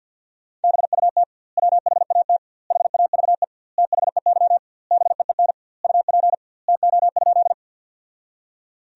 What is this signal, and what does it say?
Morse code, heard as a keyed tone, about 50 wpm, 700 Hz. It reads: BUT WHAT HAVE THEY BEEN UP TO?
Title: 2x Word-Spacing - Sentences from Top 100 Words - 50wpm